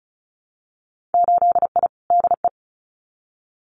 Morse code, heard as a keyed tone, about 34 wpm, 700 Hz. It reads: 8I DE
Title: Winter Field Day Exchanges Encore 35wpm